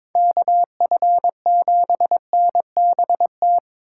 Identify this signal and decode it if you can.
Morse code, heard as a keyed tone, about 26 wpm, 700 Hz. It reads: KF7NBT
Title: Speed Racing - USA 2x3 Call Signs 22wpm